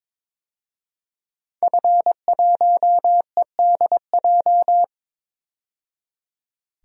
Morse code, heard as a keyed tone, about 22 wpm, 700 Hz. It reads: F1EDJ
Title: DXCC-100 Call Signs - 5 Characters - 22wpm